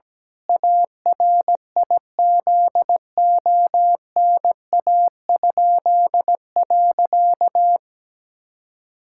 Morse code, heard as a keyed tone, about 17 wpm, 700 Hz. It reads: ARIZONA?.
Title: Pangram Challenge v2 - 17wpm